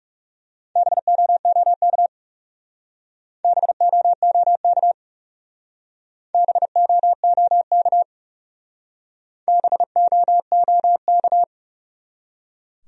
Morse code, heard as a keyed tone, about 36 wpm, 700 Hz. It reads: BOOK BOOK BOOK BOOK
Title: Speed-Racing Top 500 Words 30wpm